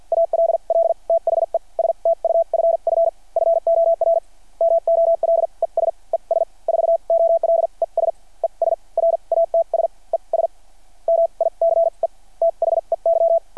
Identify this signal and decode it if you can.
Morse code, heard as a keyed tone, about 53 wpm, spaced as for 40 wpm, 660 Hz. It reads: ALL THE STUFF FOR MOLES ES 4OLES ES RATS ES MICE THEY
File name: archived / CRSnet